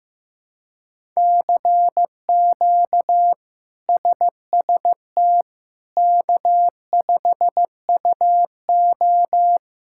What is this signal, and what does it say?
Morse code, heard as a keyed tone, about 15 wpm, 700 Hz. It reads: CQ SST K5UO
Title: K1USN SST QSO Practice - 15wpm